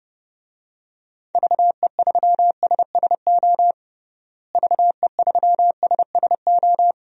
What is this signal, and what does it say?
Morse code, heard as a keyed tone, about 30 wpm, 700 Hz. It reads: VE3SSO VE3SSO
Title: DXCC-100 Call Signs - 6 Characters - Repeated 3x - 30wpm